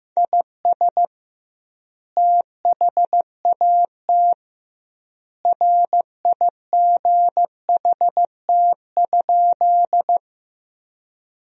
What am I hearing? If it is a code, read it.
Morse code, heard as a keyed tone, about 15 wpm, 700 Hz. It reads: IS THAT RIGHT?
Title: 2x Word-Spacing - Sentences from Top 200 Words - 15wpm